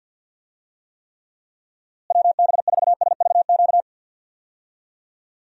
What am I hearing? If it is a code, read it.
Morse code, heard as a keyed tone, about 46 wpm, 700 Hz. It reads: WB4SVQ
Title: USA Call Signs - 6 Characters - Repeated 3x - 50wpm